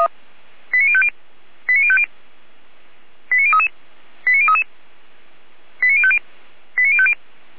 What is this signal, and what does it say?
Five-tone selective call (ZVEI), heard as quick runs of five tones, tones of about 70 ms. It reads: 89050 89050 89033 89033 89050 89050